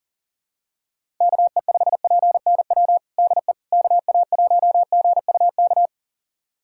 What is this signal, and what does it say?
Morse code, heard as a keyed tone, about 40 wpm, 700 Hz. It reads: KE5PNW DE KA1GUK
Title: USA Callsign DE Callsign - 6 Characters - 40wpm